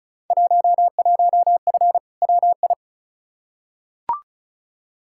Morse code, heard as a keyed tone, about 36 wpm, 690 Hz. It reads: 11F WI
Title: Field Day Exchanges Encore - 35wpm